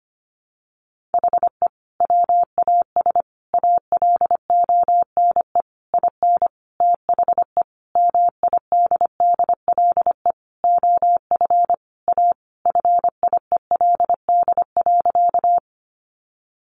Morse code, heard as a keyed tone, about 25 wpm, 700 Hz. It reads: HE WAS ALONE IN THE MIDDLE OF A FIELD.
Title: Sentences from Top 1000 Words 25wpm